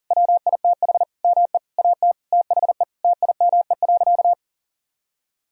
Morse code, heard as a keyed tone, about 40 wpm, 700 Hz. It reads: WITH ME AT THE TIME.
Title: Sentences from Top 1000 Words - Review Part 4 - 40wpm